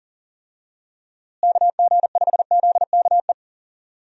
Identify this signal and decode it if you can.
Morse code, heard as a keyed tone, about 40 wpm, 700 Hz. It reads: KG5ZKE
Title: USA Call Signs - 6 Characters - Repeated 2x - 40wpm